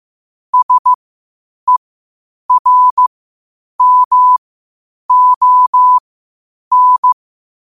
Morse code, heard as a keyed tone, about 15 wpm, spaced as for 8 wpm, 1000 Hz. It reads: SERMON
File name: qotdfiles / 8wpm